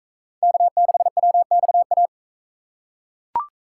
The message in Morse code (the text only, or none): K6WXA